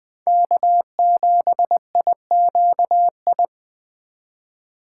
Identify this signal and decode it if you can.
Morse code, heard as a keyed tone, about 20 wpm, 700 Hz. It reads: K7IQI